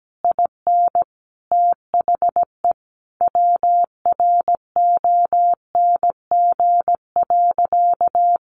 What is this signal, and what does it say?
Morse code, heard as a keyed tone, about 17 wpm, 700 Hz. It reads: IN THE WRONG.
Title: Sentences from Top 1200 Words 17wpm